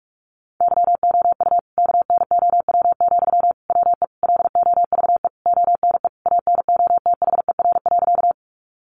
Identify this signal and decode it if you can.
Morse code, heard as a keyed tone, about 44 wpm, 700 Hz. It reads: YOU KNOW, WE LOVE ONE ANOTHER.
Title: Sentences from Top 600 Words 45wpm